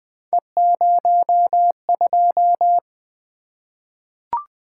Morse code, heard as a keyed tone, about 20 wpm, 700 Hz. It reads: E02
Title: Three Letter-Number Combinations - 20wpm